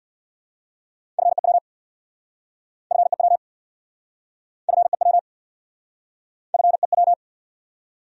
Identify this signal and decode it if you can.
Morse code, heard as a keyed tone, about 67 wpm, 700 Hz. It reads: FEW FEW FEW FEW